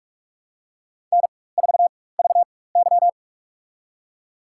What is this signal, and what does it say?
Morse code, heard as a keyed tone, about 46 wpm, spaced as for 21 wpm, 700 Hz. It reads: N4VY
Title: USA Call Signs - 4x Character Spacing - 45wpm